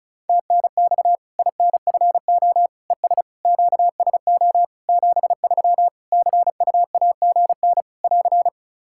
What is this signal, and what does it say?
Morse code, heard as a keyed tone, about 36 wpm, 700 Hz. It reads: TNX INFO ES QSO 73 CUAGN <AR>
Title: Practice QSO - Exchange 4 - Ending - 35wpm